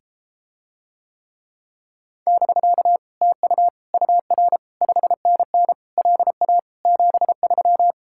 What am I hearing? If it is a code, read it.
Morse code, heard as a keyed tone, about 33 wpm, 700 Hz. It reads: <BK> TU UR 5NN LA 73